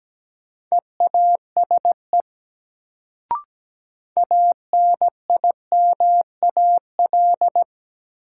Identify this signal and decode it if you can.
Morse code, heard as a keyed tone, about 17 wpm, 700 Hz. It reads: EASE ANIMAL
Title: Sentences from Top 1000 Words - Review Part 1 - New Word Drill - 17wpm